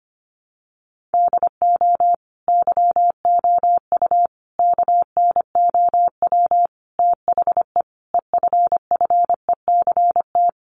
Morse code, heard as a keyed tone, about 25 wpm, 700 Hz. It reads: DO YOU KNOW THE EFFECT
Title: Sentences from Top 1400 Words 25wpm